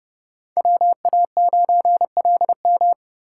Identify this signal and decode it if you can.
Morse code, heard as a keyed tone, about 33 wpm, 700 Hz. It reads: WA9LM